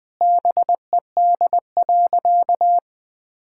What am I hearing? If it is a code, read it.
Morse code, heard as a keyed tone, about 21 wpm, 700 Hz. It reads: BED.
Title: Sentences from Top 1000 Words - Review Part 6 - 20wpm